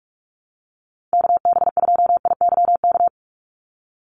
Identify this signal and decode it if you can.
Morse code, heard as a keyed tone, about 44 wpm, 700 Hz. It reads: KB2IYK
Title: USA Call Signs 45wpm